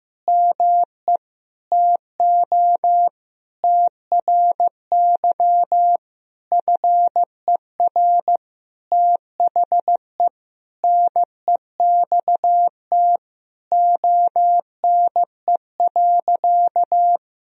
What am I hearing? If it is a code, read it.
Morse code, heard as a keyed tone, about 15 wpm, 700 Hz. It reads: ME TO TRY FER THE NEXT ONE.
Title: Sentences from Top 400 Words and Abbreviations 15wpm